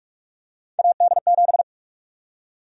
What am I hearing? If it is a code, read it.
Morse code, heard as a keyed tone, about 46 wpm, 700 Hz. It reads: AD7
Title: USA Call Sign Prefixes - 45wpm